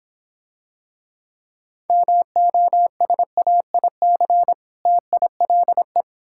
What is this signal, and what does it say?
Morse code, heard as a keyed tone, about 23 wpm, 700 Hz. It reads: MOSAIC TILE
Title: Difficult Words - Speed-Racing with Hints - 20wpm